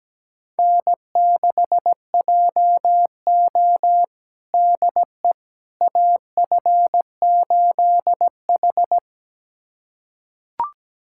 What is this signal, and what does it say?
Morse code, heard as a keyed tone, about 17 wpm, 700 Hz. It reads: N6JO DE AF8H
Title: USA Callsign DE Callsign - 4 Characters - 17wpm